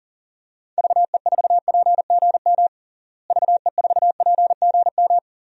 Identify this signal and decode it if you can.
Morse code, heard as a keyed tone, about 40 wpm, 700 Hz. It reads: VE4PGM VE4PGM
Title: DXCC-100 Call Signs - 6 Characters - Repeated 2x - 40wpm